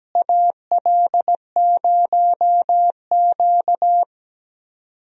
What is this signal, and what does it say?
Morse code, heard as a keyed tone, about 17 wpm, 700 Hz. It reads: AL0Q
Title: USA Call Signs - 4 Characters - Repeated 3x - 17wpm